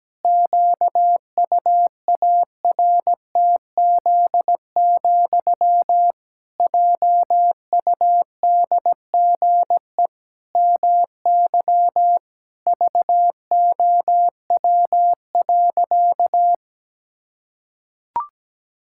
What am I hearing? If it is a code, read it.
Morse code, heard as a keyed tone, about 17 wpm, 700 Hz. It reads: QUARTZ, JUDGE MY VOW.
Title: Pangram Challenge v2 - 17wpm